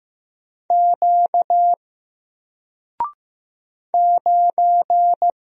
Morse code, heard as a keyed tone, about 15 wpm, 700 Hz. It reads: Q 9
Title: Single Letter-Number Alt 15wpm